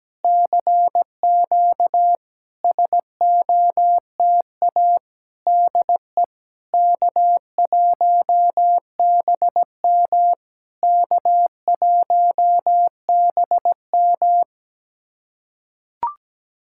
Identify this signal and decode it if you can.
Morse code, heard as a keyed tone, about 17 wpm, 690 Hz. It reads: CQ SOTA DE K1BM K1BM
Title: SOTA QSO Practice 17wpm